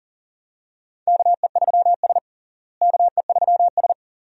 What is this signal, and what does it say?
Morse code, heard as a keyed tone, about 40 wpm, 700 Hz. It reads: KE3S KE3S
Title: USA Call Signs - 4 Characters - Repeated 2x - 40wpm